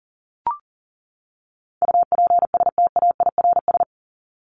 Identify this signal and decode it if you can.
Morse code, heard as a keyed tone, about 40 wpm, 700 Hz. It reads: UPSTAIRS
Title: Top 1751-2000 Words 40wpm